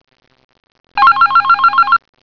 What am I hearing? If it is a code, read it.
There is no code.